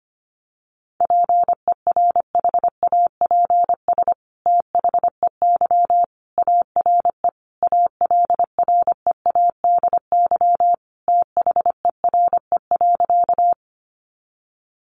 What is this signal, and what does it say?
Morse code, heard as a keyed tone, about 25 wpm, 700 Hz. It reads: PERHAPS THEY ARE ALREADY THERE.
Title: Sentences from Top 700 Words 25wpm